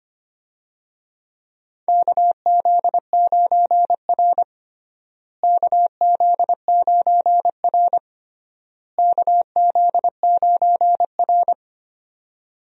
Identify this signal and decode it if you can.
Morse code, heard as a keyed tone, about 25 wpm, 700 Hz. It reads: KZ9R KZ9R KZ9R
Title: USA Call Signs - 4 Characters - Repeated 3x - 25wpm